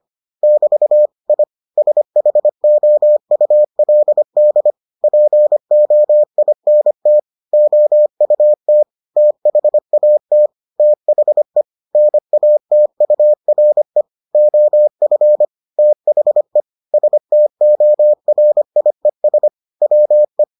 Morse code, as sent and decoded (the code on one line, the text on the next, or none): -...- / .. / ... .... --- ..- .-.. -.. / .--. --- .. -. - / --- ..- - / - .... .- - / - .... / -. .- - ..- .-. / --- ..-. / - .... / ... - --- .-. .. ... / .-- .
<BT> I SHOULD POINT OUT THAT THE NATURE OF THE STORIES WE